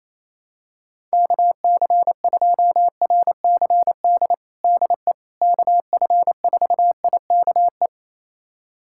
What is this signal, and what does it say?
Morse code, heard as a keyed tone, about 28 wpm, 700 Hz. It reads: KC2RCD DE KF4IKE
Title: USA Callsign DE Callsign 28wpm